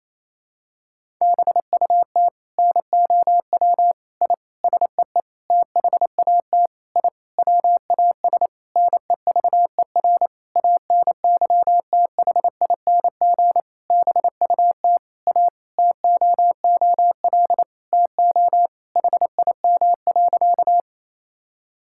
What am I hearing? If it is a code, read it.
Morse code, heard as a keyed tone, about 28 wpm, 700 Hz. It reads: BUT NOW I SEE THAT I WAS NEVER ANYTHING BUT A TOOL TO HIM.
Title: Sentences from Top 1000 Words - Review Part 6 - 28wpm